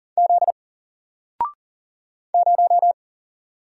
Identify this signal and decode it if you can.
Morse code, heard as a keyed tone, about 40 wpm, 700 Hz. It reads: Z 0
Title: Single Letter-Number Alt 40wpm